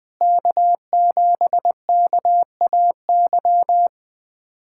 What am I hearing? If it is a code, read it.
Morse code, heard as a keyed tone, about 20 wpm, 700 Hz. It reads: K7KAY
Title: USA Callsign DE Callsign - 5 Characters - 20wpm